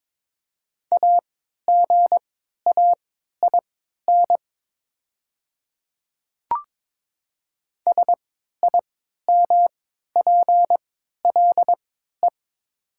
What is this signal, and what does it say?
Morse code, heard as a keyed tone, about 22 wpm, spaced as for 12 wpm, 700 Hz. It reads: AGAIN SIMPLE
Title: 3x Character Spacing - Top 500 Words - 22wpm